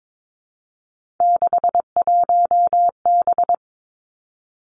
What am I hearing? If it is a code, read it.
Morse code, heard as a keyed tone, about 22 wpm, 700 Hz. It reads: 61B
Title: Three Challenging Letter-Number Combinations 22wpm